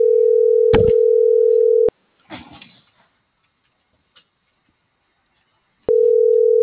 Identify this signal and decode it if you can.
Ringback tone (North America), 440 + 480 Hz.